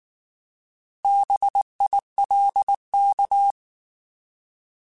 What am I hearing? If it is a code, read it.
Morse code, heard as a keyed tone, about 19 wpm, 800 Hz. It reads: BILK
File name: Alphabetical / Alphabetical mp3s / Bs @18